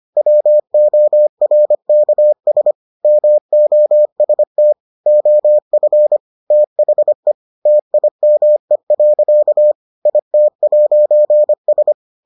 Morse code, heard as a keyed tone, about 25 wpm, 600 Hz. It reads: WORKS MOST OF THE TIME. IT'S